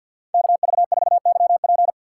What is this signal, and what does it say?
Morse code, heard as a keyed tone, about 58 wpm, 700 Hz. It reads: KV4YP
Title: Speed Racing - USA Call Signs 50wpm